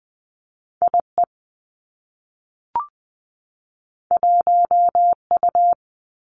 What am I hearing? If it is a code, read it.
Morse code, heard as a keyed tone, about 20 wpm, 700 Hz. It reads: I E 1 U
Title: Two Challenging Letter-Number Combinations 20wpm